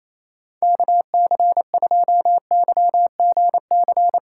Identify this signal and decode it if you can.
Morse code, heard as a keyed tone, about 28 wpm, 700 Hz. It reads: KC2YGC
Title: USA Call Signs - 4x Character Spacing - 28wpm